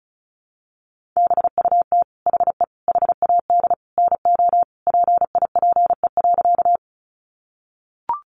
Morse code, heard as a keyed tone, about 34 wpm, 700 Hz. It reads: BUT HE HAD NO PIPE.
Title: Sentences from Top 1500 Words 35wpm